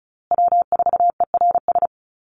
Morse code, heard as a keyed tone, about 34 wpm, 700 Hz. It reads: W4ERS
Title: USA Call Signs Repeated 2x 35wpm